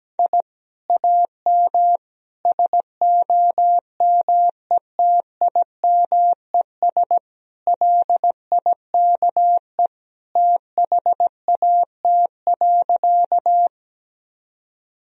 Morse code, heard as a keyed tone, about 17 wpm, 700 Hz. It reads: I AM SOMETIMES LIKE THAT.
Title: Sentences from Top 1000 Words - Review Part 3 - 17wpm